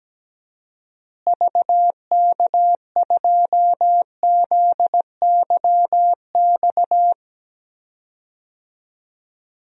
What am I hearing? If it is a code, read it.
Morse code, heard as a keyed tone, about 17 wpm, 700 Hz. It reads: VK2ZYX
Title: DXCC-100 Call Signs - 6 Characters - Repeated 3x - 17wpm